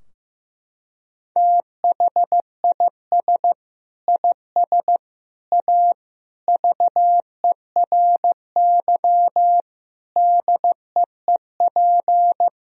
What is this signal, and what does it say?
Morse code, heard as a keyed tone, about 15 wpm, 700 Hz. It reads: THIS IS A VERY DEEP
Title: Sentences from Top 1000 Words - Review Part 3 - 15wpm